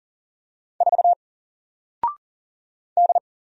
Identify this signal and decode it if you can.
Morse code, heard as a keyed tone, about 40 wpm, 700 Hz. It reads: 4 D
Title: Single Letter-Number Alt 40wpm